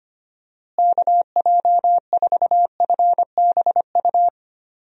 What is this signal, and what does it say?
Morse code, heard as a keyed tone, about 25 wpm, 700 Hz. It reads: KJ4FBU
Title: USA Call Signs - 6 Characters - Repeated 2x - 25wpm